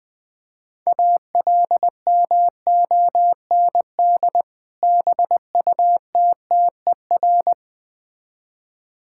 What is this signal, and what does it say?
Morse code, heard as a keyed tone, about 21 wpm, 700 Hz. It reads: ALMOND BUTTER
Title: Difficult Words with Hints - 20wpm